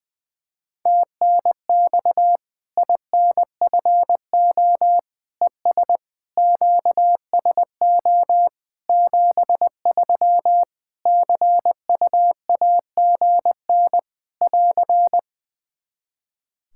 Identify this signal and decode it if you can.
Morse code, heard as a keyed tone, about 20 wpm, 700 Hz. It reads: TNX INFO ES QSO 73 CUAGN <AR>
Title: K1USN SST QSO Practice 20wpm